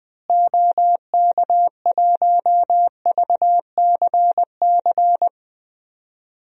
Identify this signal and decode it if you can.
Morse code, heard as a keyed tone, about 21 wpm, 700 Hz. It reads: OK1VCC
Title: DXCC-100 Call Signs - 6 Characters - Repeated 2x - 20wpm